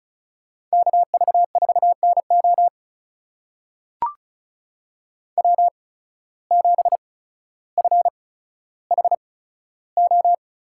Morse code, heard as a keyed tone, about 34 wpm, spaced as for 15 wpm, 700 Hz. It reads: KV4NO W7FHO
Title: USA Call Signs - 5 Characters - 8x Character Spacing - 35wpm